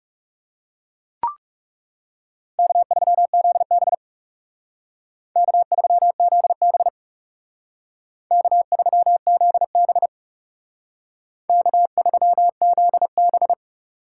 Morse code, heard as a keyed tone, about 35 wpm, 700 Hz. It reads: K3ZB K3ZB K3ZB K3ZB
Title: Speed Racing - USA 1x2 Call Signs 30wpm